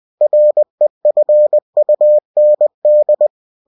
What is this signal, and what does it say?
Morse code, heard as a keyed tone, about 20 wpm, 600 Hz. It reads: REFUND
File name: cw-books / winnie-the-pooh / winnie-20wpm-page-0117